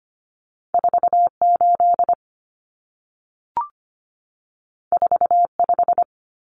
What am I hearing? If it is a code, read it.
Morse code, heard as a keyed tone, about 25 wpm, 700 Hz. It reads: 48 45
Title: Common Cut Numbers - 2-Digit 25wpm